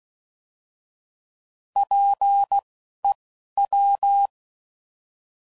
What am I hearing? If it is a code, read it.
Morse code, heard as a keyed tone, about 16 wpm, spaced as for 12 wpm, 800 Hz. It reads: PEW
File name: Prefixes / By First Letter / Ps